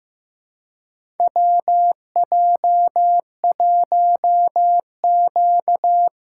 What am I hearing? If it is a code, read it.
Morse code, heard as a keyed tone, about 15 wpm, 700 Hz. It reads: WJ1Q
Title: USA Call Signs - 3x Character Spacing - 15wpm